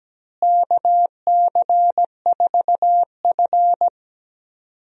Morse code, heard as a keyed tone, about 17 wpm, 700 Hz. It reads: KC4F
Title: USA Call Signs - 4 Characters - Repeated 2x - 17wpm